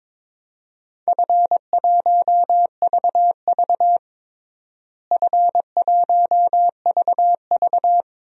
Morse code, heard as a keyed tone, about 21 wpm, 700 Hz. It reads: F1VV F1VV
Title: DXCC-100 Call Signs - 4 Characters - Repeated 3x - 22wpm